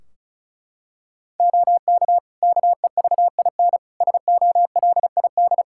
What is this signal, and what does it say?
Morse code, heard as a keyed tone, about 36 wpm, 700 Hz. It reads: OK KEVIN SOLID
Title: Practice QSO - Exchange 3 - More Info - 35wpm